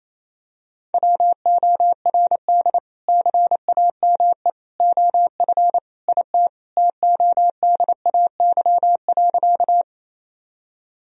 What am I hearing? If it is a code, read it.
Morse code, heard as a keyed tone, about 28 wpm, 700 Hz. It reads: WORD CAME OF IT TODAY.
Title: Sentences from Top 1000 Words - Review Part 4 - 28wpm